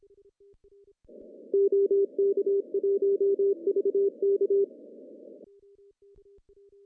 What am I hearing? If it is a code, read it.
Morse code, heard as a keyed tone, about 26 wpm, 400 Hz. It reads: OK1VK
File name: CW / cqww-cw-k5tr